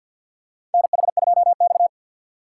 Morse code, heard as a keyed tone, about 48 wpm, 700 Hz. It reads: NH2X
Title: USA Call Signs - 4x Character Spacing - 50wpm